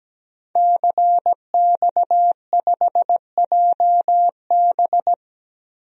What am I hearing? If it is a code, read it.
Morse code, heard as a keyed tone, about 17 wpm, 700 Hz. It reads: CX5JB
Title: DXCC-100 Call Signs - 5 Characters - Repeated 3x - 17wpm